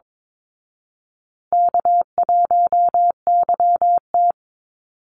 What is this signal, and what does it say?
Morse code, heard as a keyed tone, about 22 wpm, 700 Hz. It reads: K1YT
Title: Easy Call Signs Repeated 2x - 22wpm